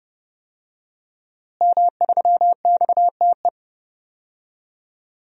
Morse code, heard as a keyed tone, about 32 wpm, 700 Hz. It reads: M3XTE
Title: DXCC-100 Call Signs - 5 Characters - Repeated 3x - 30wpm